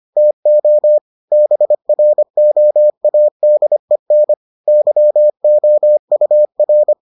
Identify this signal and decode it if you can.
Morse code, heard as a keyed tone, about 25 wpm, 600 Hz. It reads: TO BROADEN YOUR